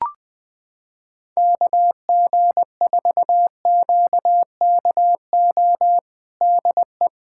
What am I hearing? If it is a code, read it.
Morse code, heard as a keyed tone, about 21 wpm, 690 Hz. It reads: KG4QKO DE
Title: USA Callsign DE Callsign 20wpm